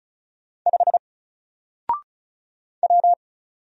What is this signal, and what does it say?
Morse code, heard as a keyed tone, about 34 wpm, 700 Hz. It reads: 5 W
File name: Single-Letter-Number-35wpm